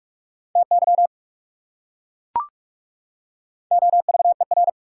Morse code, heard as a keyed tone, about 37 wpm, 700 Hz. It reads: TY OVER
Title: Speed-Racing Prefixes and Suffixes 30wpm